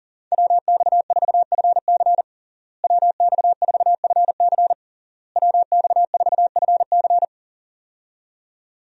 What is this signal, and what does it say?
Morse code, heard as a keyed tone, about 40 wpm, 700 Hz. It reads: WX4FC WX4FC WX4FC
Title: USA Call Signs - 5 Characters - Repeated 3x - 40wpm